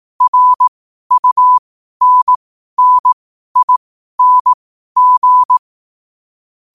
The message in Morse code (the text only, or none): RUNNING